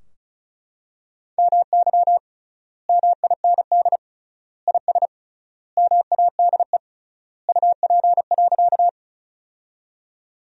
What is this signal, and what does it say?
Morse code, heard as a keyed tone, about 34 wpm, 700 Hz. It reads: MY MIND IS MADE UP.
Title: 3x Word-Spacing - Sentences from Top 500 Words - 35wpm